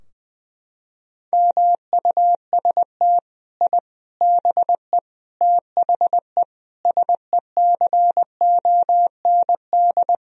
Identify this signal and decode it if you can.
Morse code, heard as a keyed tone, about 20 wpm, 700 Hz. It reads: MUST I BE THE SECOND
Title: Sentences from Top 1000 Words - Review Part 4 - 20wpm